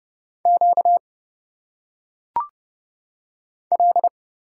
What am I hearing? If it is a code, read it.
Morse code, heard as a keyed tone, about 29 wpm, 700 Hz. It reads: Q L
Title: Single Letters - Phonetic Pronunciation 30wpm